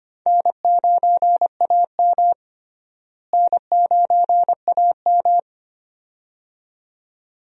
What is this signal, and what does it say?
Morse code, heard as a keyed tone, about 25 wpm, 700 Hz. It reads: N9AM N9AM
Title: USA Call Signs - 4 Characters - Repeated 2x - 25wpm